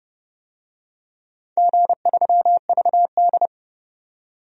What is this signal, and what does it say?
Morse code, heard as a keyed tone, about 32 wpm, 700 Hz. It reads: G3VD
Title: DXCC-100 Call Signs - 4 Characters - 30wpm